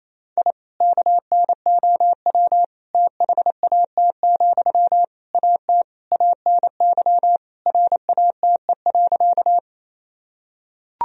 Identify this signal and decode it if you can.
Morse code, heard as a keyed tone, about 28 wpm, 700 Hz. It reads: I KNOW THAT, AT ANY RATE.
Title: Sentences from Top 1000 Words - Review Part 4 - 28wpm